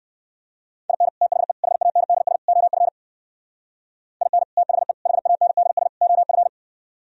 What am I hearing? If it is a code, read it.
Morse code, heard as a keyed tone, about 58 wpm, 700 Hz. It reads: IN THE HANDS OF IN THE HANDS OF